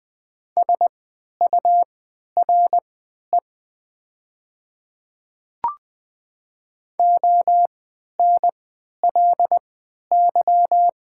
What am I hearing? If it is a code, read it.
Morse code, heard as a keyed tone, about 21 wpm, spaced as for 12 wpm, 700 Hz. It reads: SURE ONLY
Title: 3x Character Spacing - Top 300 Words - 20wpm